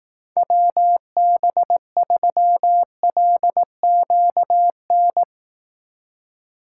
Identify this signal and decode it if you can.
Morse code, heard as a keyed tone, about 18 wpm, 700 Hz. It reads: WB3LQN
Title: Speed Racing - USA 2x3 Call Signs 15wpm